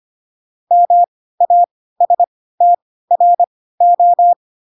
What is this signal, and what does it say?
Morse code, heard as a keyed tone, about 25 wpm, spaced as for 16 wpm, 700 Hz. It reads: MASTRO
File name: Pinocchio / Pinocchio ch2 10 25 ik1hge